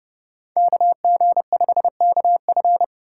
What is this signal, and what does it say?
Morse code, heard as a keyed tone, about 30 wpm, 700 Hz. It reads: KG5KF